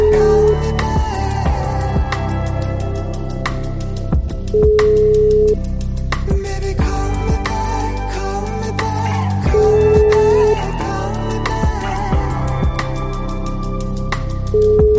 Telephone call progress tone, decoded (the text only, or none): ringback tone